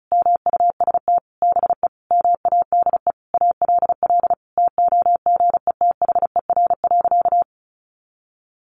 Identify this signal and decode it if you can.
Morse code, heard as a keyed tone, about 34 wpm, 700 Hz. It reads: MUST BE MADE ALL TOGETHER.
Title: Sentences from Top 1100 Words 35wpm